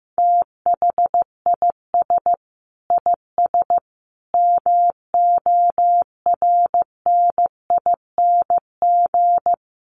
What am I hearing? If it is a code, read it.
Morse code, heard as a keyed tone, about 15 wpm, 700 Hz. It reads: THIS IS MORNING
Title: Sets of 5 Words Encore2 15wpm